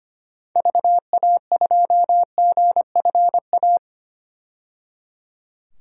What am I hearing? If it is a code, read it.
Morse code, heard as a keyed tone, about 25 wpm, 700 Hz. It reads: VA2GFA
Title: DXCC-100 Call Signs - 6 Characters - 25wpm